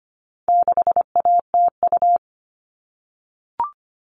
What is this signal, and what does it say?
Morse code, heard as a keyed tone, about 25 wpm, 700 Hz. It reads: 6ATU
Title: All Cut Numbers - 4-Digit 25wpm